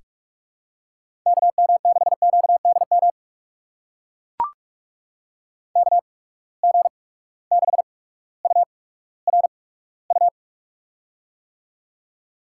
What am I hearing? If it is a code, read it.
Morse code, heard as a keyed tone, about 44 wpm, spaced as for 19 wpm, 700 Hz. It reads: KM6QDM KG6URU
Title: USA Call Signs - 6 Characters - 8x Character Spacing - 45wpm